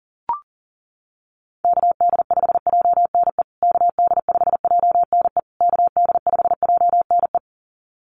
Morse code, heard as a keyed tone, about 40 wpm, 700 Hz. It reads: KD5JNE KD5JNE KD5JNE
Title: USA Call Signs Repeated 3x 40wpm